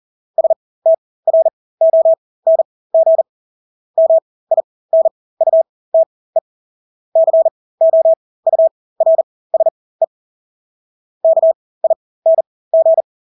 Morse code, heard as a keyed tone, about 40 wpm, spaced as for 20 wpm, 650 Hz. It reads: STRONG MINUTE COURSE KING
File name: Engelske ord 40-20 wpm